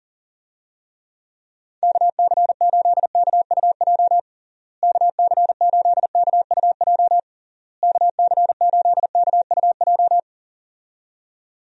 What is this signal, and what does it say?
Morse code, heard as a keyed tone, about 40 wpm, 700 Hz. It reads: KC8KUJ KC8KUJ KC8KUJ
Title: USA Call Signs - 6 Characters - Repeated 3x - 40wpm